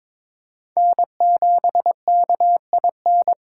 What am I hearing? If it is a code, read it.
Morse code, heard as a keyed tone, about 26 wpm, 700 Hz. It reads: N7KIN
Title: Speed Racing - USA 1x3 Call Signs 22wpm